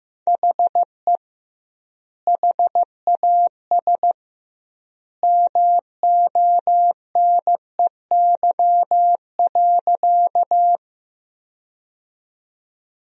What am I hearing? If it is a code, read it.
Morse code, heard as a keyed tone, about 15 wpm, 700 Hz. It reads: HE HAS MONEY.
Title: 2x Word-Spacing - Sentences from Top 400 Words - 15wpm